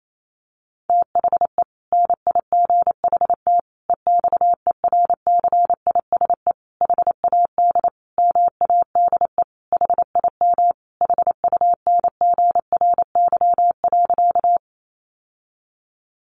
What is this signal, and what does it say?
Morse code, heard as a keyed tone, about 28 wpm, 700 Hz. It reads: THE NIGHT EXERCISE HAD MADE HIM HUNGRY.
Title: Sentences from Top 1400 Words 28wpm